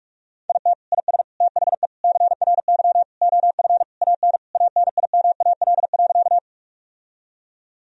Morse code, heard as a keyed tone, about 46 wpm, 700 Hz. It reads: IT IS THE CRY OF AN ANIMAL.
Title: Sentences from Top 1000 Words - Review Part 2 - 45wpm